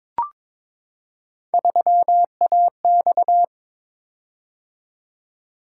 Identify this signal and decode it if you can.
Morse code, heard as a keyed tone, about 22 wpm, 700 Hz. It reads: 3AX
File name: practice / 22wpm / Three Letter-Number Combinations - 22wpm